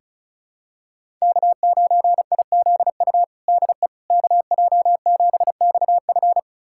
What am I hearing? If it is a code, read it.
Morse code, heard as a keyed tone, about 34 wpm, 700 Hz. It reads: K9IZU DE KJ7XF
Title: USA Callsign DE Callsign - 5 Characters - 35wpm